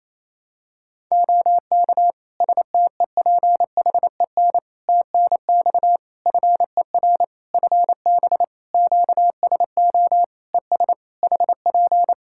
Morse code, heard as a keyed tone, about 28 wpm, 700 Hz. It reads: OK STEPHEN TNX FER FB QSO ES HP
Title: Practice QSO - Exchange 4 - Ending - 28wpm